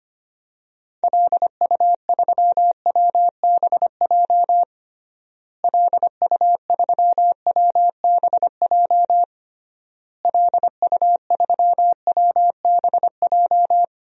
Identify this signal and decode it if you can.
Morse code, heard as a keyed tone, about 25 wpm, 700 Hz. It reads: LU3WBJ LU3WBJ LU3WBJ
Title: DXCC-100 Call Signs - 6 Characters - Repeated 3x - 25wpm